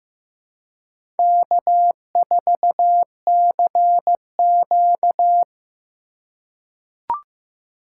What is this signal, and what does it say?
Morse code, heard as a keyed tone, about 15 wpm, 700 Hz. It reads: K4CQ
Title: USA Call Signs - 4 Characters - 4x Character Spacing - 15wpm